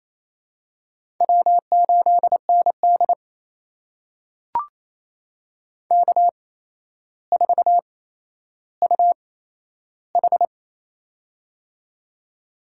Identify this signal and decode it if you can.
Morse code, heard as a keyed tone, about 28 wpm, spaced as for 12 wpm, 700 Hz. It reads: W8ND K4UH
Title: USA Call Signs - 4 Characters - 8x Character Spacing - 28wpm